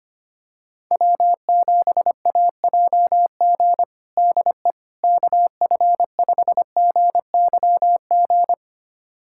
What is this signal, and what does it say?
Morse code, heard as a keyed tone, about 25 wpm, 700 Hz. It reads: W7AJG DE KF5GYG
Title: USA Callsign DE Callsign 25wpm